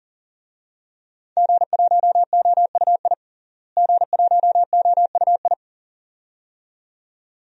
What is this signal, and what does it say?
Morse code, heard as a keyed tone, about 40 wpm, 700 Hz. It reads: G1OUI G1OUI
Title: DXCC-100 Call Signs - 5 Characters - Repeated 2x - 40wpm